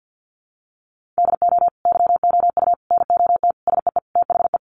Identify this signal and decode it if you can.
Morse code, heard as a keyed tone, about 50 wpm, 700 Hz. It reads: DO YOU NOT SEE THE
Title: Sentences from Top 800 Words 50wpm